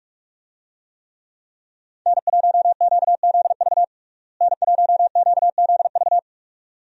Morse code, heard as a keyed tone, about 46 wpm, 700 Hz. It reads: N1QZV N1QZV
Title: USA Call Signs - 5 Characters - Repeated 2x - 45wpm